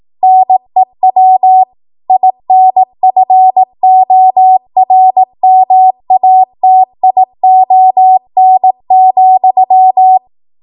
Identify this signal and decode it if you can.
Morse code, heard as a keyed tone, about 18 wpm, 750 Hz. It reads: NEW INFORMATION,